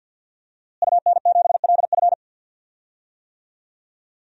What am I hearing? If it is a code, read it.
Morse code, heard as a keyed tone, about 52 wpm, 700 Hz. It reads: UN7LF
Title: DXCC-100 Call Signs - 5 Characters - Repeated 3x - 50wpm